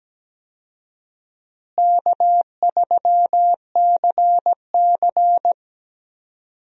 Morse code, heard as a keyed tone, about 17 wpm, 700 Hz. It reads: K3CC
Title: USA Call Signs - 4 Characters - Repeated 2x - 17wpm